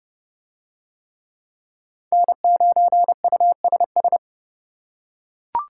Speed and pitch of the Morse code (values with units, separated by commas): 29 wpm, 700 Hz